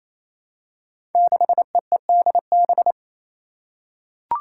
Morse code, heard as a keyed tone, about 28 wpm, 700 Hz. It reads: 6EEDB
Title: Five Challenging Letter-Number Combinations 28wpm